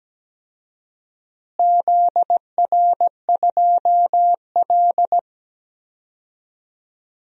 Morse code, heard as a keyed tone, about 17 wpm, 700 Hz. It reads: ZR2L
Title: DXCC-100 Call Signs - 4 Characters - Repeated 3x - 17wpm